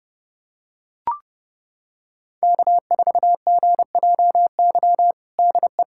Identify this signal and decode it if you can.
Morse code, heard as a keyed tone, about 31 wpm, 700 Hz. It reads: K4GJY DE
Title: USA Callsign DE Callsign - 5 Characters - 30wpm